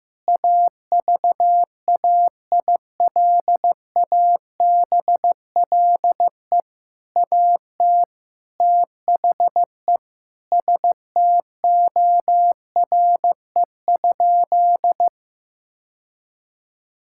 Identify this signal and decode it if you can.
Morse code, heard as a keyed tone, about 15 wpm, 700 Hz. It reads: AVAILABLE AT THE STORE?
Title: Sentences from Top 1000 Words - Review Part 6 - 15wpm